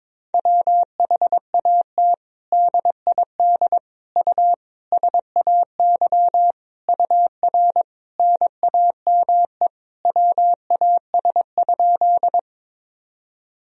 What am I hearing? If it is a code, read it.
Morse code, heard as a keyed tone, about 22 wpm, 700 Hz. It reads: WHAT DID U SAY UR NAME WAS?
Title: Sentences from Top 200 Words and Abbreviations 22wpm